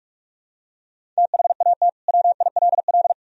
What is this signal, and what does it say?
Morse code, heard as a keyed tone, about 41 wpm, 700 Hz. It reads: THAT WILL
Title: Sentences from Top 100 Words and Abbreviations 45wpm